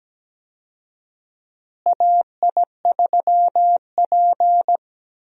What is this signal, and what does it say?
Morse code, heard as a keyed tone, about 17 wpm, 700 Hz. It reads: AI3P
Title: USA Call Signs - 4 Characters - Repeated 3x - 17wpm